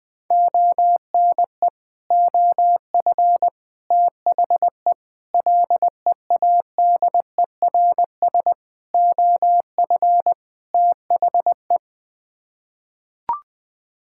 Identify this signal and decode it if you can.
Morse code, heard as a keyed tone, about 20 wpm, 700 Hz. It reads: ONE OF THE LEADERS OF THE
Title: Sets of 6 Words - 20wpm